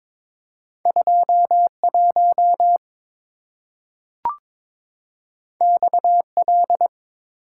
Morse code, heard as a keyed tone, about 22 wpm, 690 Hz. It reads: 21 XL
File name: practice / 22wpm / Two Letter-Number Combinations - 22wpm